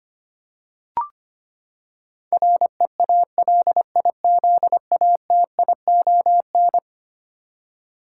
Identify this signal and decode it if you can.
Morse code, heard as a keyed tone, about 25 wpm, 700 Hz. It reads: REALIZATION
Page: morse-code-ninja-practice-25wpm